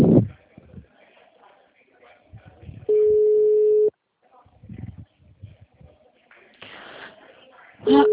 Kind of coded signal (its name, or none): telephone call progress tone